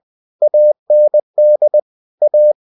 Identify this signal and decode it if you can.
Morse code, heard as a keyed tone, about 20 wpm, 600 Hz. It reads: AND A
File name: cw-books / winnie-the-pooh / winnie-20wpm-page-0028